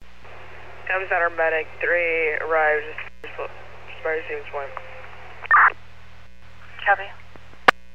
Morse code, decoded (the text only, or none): none